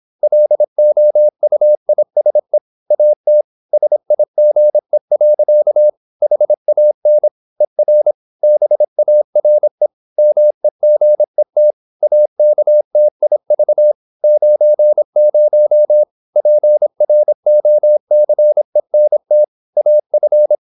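Morse code, heard as a keyed tone, about 26 wpm, 600 Hz. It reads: LOUISE AT SIGE. HAN ER BARE MEGET AKTIV 90 PROCENT AF
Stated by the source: High-speed (QRQ) Morse at 26 wpm.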